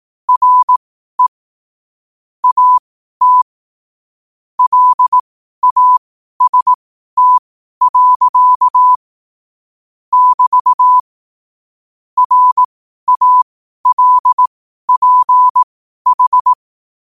Morse code, heard as a keyed tone, about 18 wpm, spaced as for 12 wpm, 1000 Hz. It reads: RE AT LAST. <BT> RALPH
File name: qotdfiles / 12wpm